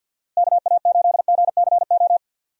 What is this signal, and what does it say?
Morse code, heard as a keyed tone, about 50 wpm, 700 Hz. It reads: KA8GCO
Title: USA Call Signs - 6 Characters - Repeated 2x - 50wpm